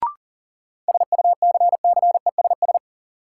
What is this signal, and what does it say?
Morse code, heard as a keyed tone, about 40 wpm, 700 Hz. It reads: SUCCESS
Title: Sentences from Top 900 Words - New Word Drill - 40wpm